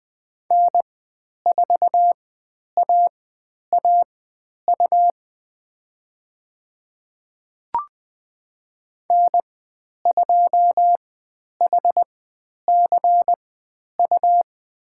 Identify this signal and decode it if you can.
Morse code, heard as a keyed tone, about 20 wpm, spaced as for 10 wpm, 700 Hz. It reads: N4AAU N2HCU
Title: 109 Lesson 18g - Letter m - Call Signs 20wpm